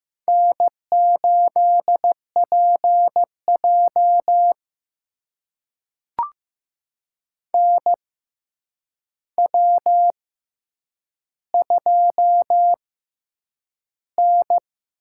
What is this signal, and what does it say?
Morse code, heard as a keyed tone, about 15 wpm, spaced as for 8 wpm, 700 Hz. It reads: N8PJ NW2N
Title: USA Call Signs - 4 Characters - 6x Character Spacing - 15wpm